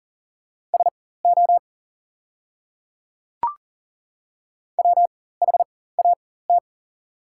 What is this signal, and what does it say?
Morse code, heard as a keyed tone, about 40 wpm, spaced as for 19 wpm, 690 Hz. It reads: SO WHAT
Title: 4x Character Spacing - Top 500 Words - 40wpm